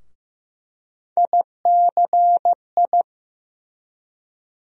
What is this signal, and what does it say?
Morse code, heard as a keyed tone, about 15 wpm, 700 Hz. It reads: ICI
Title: USA Call Sign Postfixes - 3 Letters 15wpm